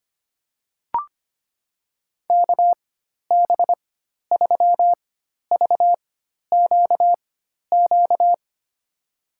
Morse code, heard as a keyed tone, about 25 wpm, spaced as for 12 wpm, 700 Hz. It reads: KB3VQQ
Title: USA Call Signs - 4x Character Spacing - 25wpm